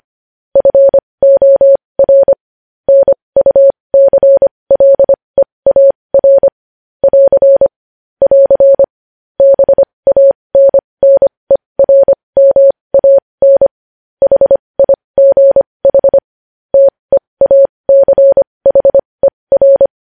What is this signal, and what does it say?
Morse code, heard as a keyed tone, about 25 wpm, spaced as for 20 wpm, 550 Hz. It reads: FOR NUCLEAR <AR> <AR> BANNERMAN HIGH TEACHER
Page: news headlines in morse code at 20 wpm